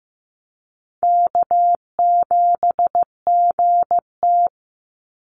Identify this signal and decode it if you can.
Morse code, heard as a keyed tone, about 15 wpm, 700 Hz. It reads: K7GT
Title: Easy Call Signs Repeated 3x - 15wpm